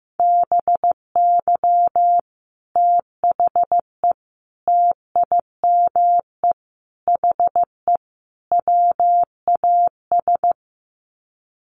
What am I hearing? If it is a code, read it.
Morse code, heard as a keyed tone, about 15 wpm, 700 Hz. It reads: BY THE TIME HE WAS
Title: Sets of 5 Words Encore 15wpm